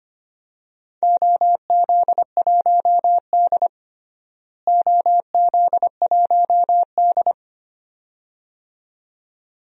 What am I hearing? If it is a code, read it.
Morse code, heard as a keyed tone, about 25 wpm, 700 Hz. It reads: OZ1D OZ1D
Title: DXCC-100 Call Signs - 4 Characters - Repeated 2x - 25wpm